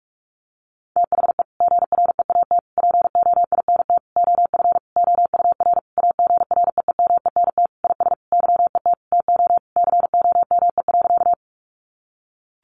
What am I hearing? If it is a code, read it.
Morse code, heard as a keyed tone, about 44 wpm, 700 Hz. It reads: THE GREAT POINT OF OUR AGREEMENT IS YET TO COME.
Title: Sentences from Top 1500 Words 45wpm